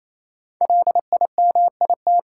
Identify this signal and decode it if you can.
Morse code, heard as a keyed tone, about 27 wpm, 700 Hz. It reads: LIMIT
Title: Sentences from Top 1500 Words - New Word Drill Alt - 28wpm